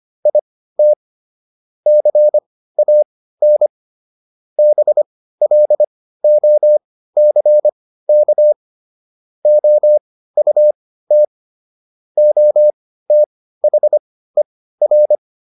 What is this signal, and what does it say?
Morse code, heard as a keyed tone, about 25 wpm, spaced as for 15 wpm, 600 Hz. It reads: IT CAN BLOCK OUT OTHER
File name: SS203 15